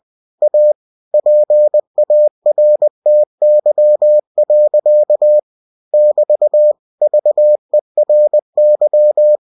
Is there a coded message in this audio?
Yes: Morse code from 0.4 to 9.4 s, heard as a keyed tone, about 20 wpm, 600 Hz. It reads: A PARTY. <BT> VERY